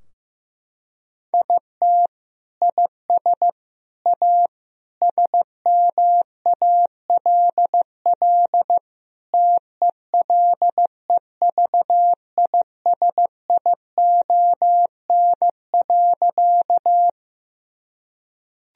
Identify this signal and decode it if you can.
Morse code, heard as a keyed tone, about 16 wpm, 700 Hz. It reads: IT IS A SMALL TELEVISION.
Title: Sentences from Top 1000 Words - Review Part 4 - 15wpm